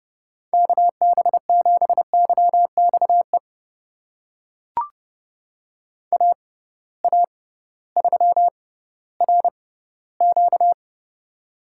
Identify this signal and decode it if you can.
Morse code, heard as a keyed tone, about 32 wpm, spaced as for 16 wpm, 700 Hz. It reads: KB7YXE AA3RQ
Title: USA Call Signs - 6x Character Spacing - 30wpm